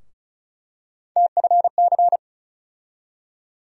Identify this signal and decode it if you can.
Morse code, heard as a keyed tone, about 36 wpm, 700 Hz. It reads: TFC
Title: USA Call Sign Postfixes - 3 Letters 35wpm